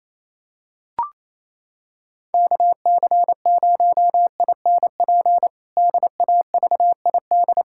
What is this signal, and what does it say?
Morse code, heard as a keyed tone, about 28 wpm, 700 Hz. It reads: KC0INP DAVID
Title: K1USN SST QSO Practice - 28wpm